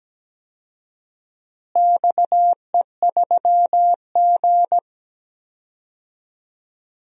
Morse code, heard as a keyed tone, about 17 wpm, 700 Hz. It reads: XE3G
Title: DXCC-100 Call Signs - 4 Characters - Repeated 2x - 17wpm